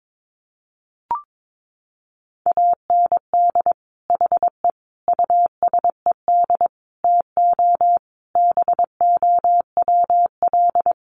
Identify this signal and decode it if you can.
Morse code, heard as a keyed tone, about 22 wpm, 700 Hz. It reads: AND HE USED TO BOWL
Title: Sentences from Top 1200 Words 22wpm